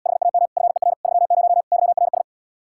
Morse code, heard as a keyed tone, about 80 wpm, 700 Hz. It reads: HAM FER 30 YRS